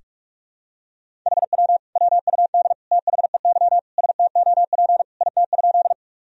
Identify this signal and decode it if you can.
Morse code, heard as a keyed tone, about 46 wpm, 700 Hz. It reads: HW WUD THEY STOP IT?
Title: Sentences from Top 300 Words and Abbreviations 45wpm